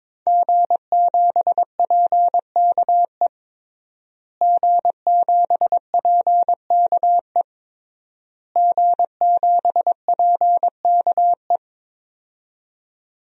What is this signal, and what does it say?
Morse code, heard as a keyed tone, about 22 wpm, 700 Hz. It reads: G7PKE G7PKE G7PKE
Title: DXCC-100 Call Signs - 5 Characters - Repeated 3x - 22wpm